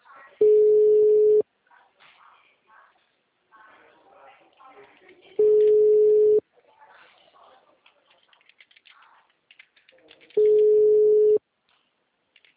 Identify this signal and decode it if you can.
Ringback tone (Europe, 425 Hz), with speech.